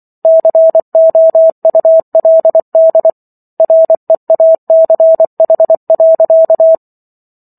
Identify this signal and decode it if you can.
Morse code, heard as a keyed tone, about 24 wpm, 650 Hz. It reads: COULD REACH.